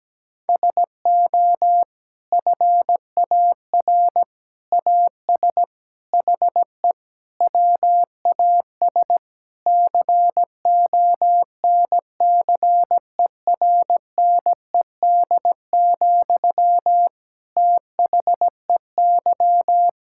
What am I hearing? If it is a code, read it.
Morse code, heard as a keyed tone, about 17 wpm, 700 Hz. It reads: SO FAR AS HE WAS CONCERNED, THEY
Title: Sentences from Top 1000 Words - Review Part 6 - 17wpm